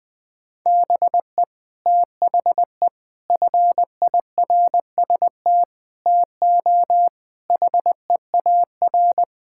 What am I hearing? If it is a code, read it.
Morse code, heard as a keyed tone, about 20 wpm, 700 Hz. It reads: BE THE FIRST TO HEAR
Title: Sets of 6 Words - 20wpm